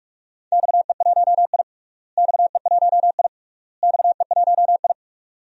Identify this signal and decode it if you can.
Morse code, heard as a keyed tone, about 42 wpm, 700 Hz. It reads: XE1I XE1I XE1I
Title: DXCC-100 Call Signs - 4 Characters - Repeated 3x - 45wpm